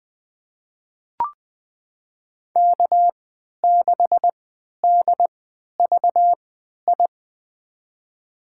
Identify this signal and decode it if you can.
Morse code, heard as a keyed tone, about 20 wpm, spaced as for 11 wpm, 700 Hz. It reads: K6DVI
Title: USA Call Signs - 3x Character Spacing - 20wpm